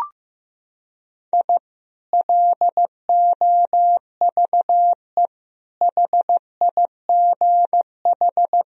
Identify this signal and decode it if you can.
Morse code, heard as a keyed tone, about 15 wpm, 700 Hz. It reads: I LOVE HIGH
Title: Sentences from Top 1000 Words - Review Part 4 - 15wpm